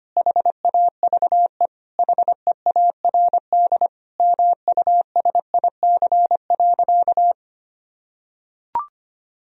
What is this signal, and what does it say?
Morse code, heard as a keyed tone, about 25 wpm, 700 Hz. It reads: HAVE HEARD MUSIC.
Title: Sentences from Top 1000 Words - Review Part 1 - 25wpm